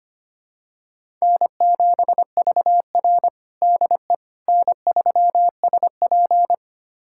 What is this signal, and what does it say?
Morse code, heard as a keyed tone, about 25 wpm, 700 Hz. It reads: N7VR DE N3SP
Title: USA Callsign DE Callsign - 4 Characters - 25wpm